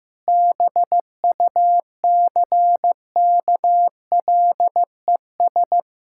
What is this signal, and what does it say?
Morse code, heard as a keyed tone, about 16 wpm, 700 Hz. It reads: BUCKLES
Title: Pangram Challenge v2 - 15wpm